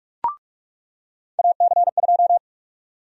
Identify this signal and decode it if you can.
Morse code, heard as a keyed tone, about 44 wpm, 700 Hz. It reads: AC2
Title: USA Call Sign Prefixes - 45wpm